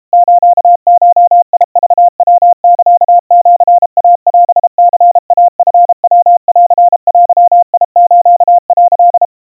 Morse code (Japanese Wabun code), heard as a keyed tone, about 32 wpm, 700 Hz. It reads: スゴクヤサシイオニイチヤンデス」
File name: kawa / Morse sample / kamyu 120 cpm 700 Hz Crispy